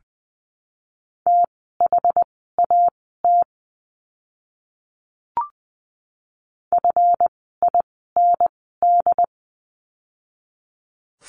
Morse code, heard as a keyed tone, about 20 wpm, spaced as for 15 wpm, 700 Hz. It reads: THAT FIND
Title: 160 Lesson 26f - Letter g - Words 20wpm